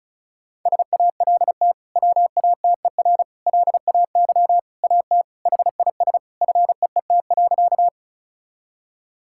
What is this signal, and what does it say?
Morse code, heard as a keyed tone, about 36 wpm, 700 Hz. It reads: SALT WATER LAY AT HIS FEET.
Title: Sentences from Top 1000 Words - Review Part 5 - 35wpm